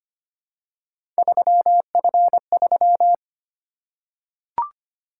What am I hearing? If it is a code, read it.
Morse code, heard as a keyed tone, about 25 wpm, 700 Hz. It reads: 3F3
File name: Three Letter-Number Combinations - 25wpm